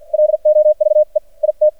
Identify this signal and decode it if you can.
Morse code, heard as a keyed tone, about 53 wpm, spaced as for 44 wpm, 610 Hz. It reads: LOVE IT
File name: archived / CRSnet